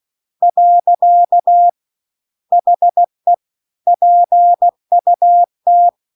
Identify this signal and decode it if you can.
Morse code, heard as a keyed tone, about 16 wpm, 700 Hz. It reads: . HE PUT